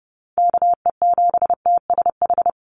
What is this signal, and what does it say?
Morse code, heard as a keyed tone, about 30 wpm, 700 Hz. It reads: KE7TSH
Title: USA Call Signs 30wpm